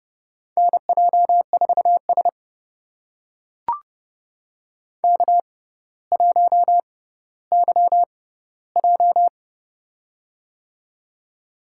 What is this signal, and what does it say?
Morse code, heard as a keyed tone, about 30 wpm, spaced as for 15 wpm, 700 Hz. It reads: NJ4S K1YJ
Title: USA Call Signs - 4 Characters - 6x Character Spacing - 30wpm